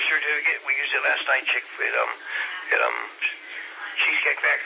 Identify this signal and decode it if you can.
Emergency Alert System (SAME) tones, heard as harsh buzzing bursts, with speech. It is heard from 1.0 to 3.9 s.